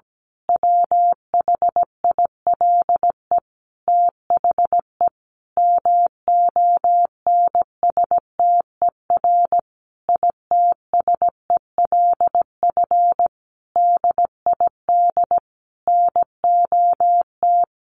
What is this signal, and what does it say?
Morse code, heard as a keyed tone, about 17 wpm, 700 Hz. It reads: WHILE THE MONSTER ITSELF DID NOT
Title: A Sound Of Thunder 17wpm